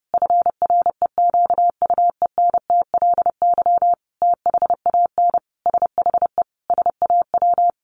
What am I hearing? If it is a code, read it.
Morse code, heard as a keyed tone, about 30 wpm, 700 Hz. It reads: FREQUENTLY THAN SHE SAW
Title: Sentences from Top 1300 Words 30wpm